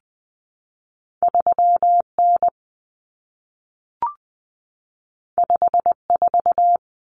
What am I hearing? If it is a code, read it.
Morse code, heard as a keyed tone, about 20 wpm, 700 Hz. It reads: 3N 54
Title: Common Cut Numbers - 2-Digit 20wpm